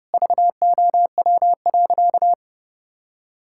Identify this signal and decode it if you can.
Morse code, heard as a keyed tone, about 32 wpm, 700 Hz. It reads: VOW.
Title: Pangram Challenge v2 - 30wpm